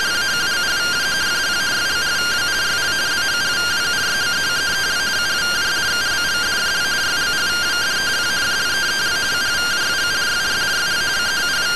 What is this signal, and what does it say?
Two-tone teleprinter signal, about 170 Hz shift.